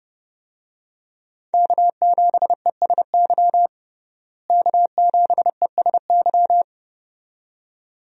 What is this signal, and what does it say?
Morse code, heard as a keyed tone, about 30 wpm, 700 Hz. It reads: K7ESY K7ESY
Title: USA Call Signs - 5 Characters - Repeated 2x - 30wpm